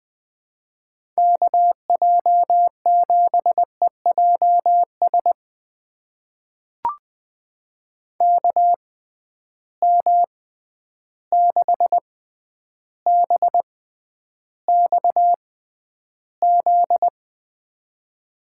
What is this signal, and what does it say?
Morse code, heard as a keyed tone, about 20 wpm, spaced as for 10 wpm, 700 Hz. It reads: KJ7EJS KM6BXZ
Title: USA Call Signs - 6 Characters - 6x Character Spacing - 20wpm